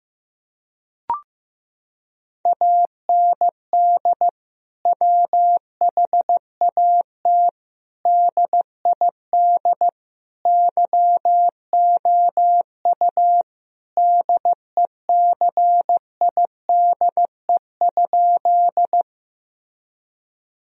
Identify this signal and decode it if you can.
Morse code, heard as a keyed tone, about 15 wpm, 700 Hz. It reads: AND WHAT DID YOU DECIDE?
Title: Sentences from Top 1000 Words - Review Part 3 - 15wpm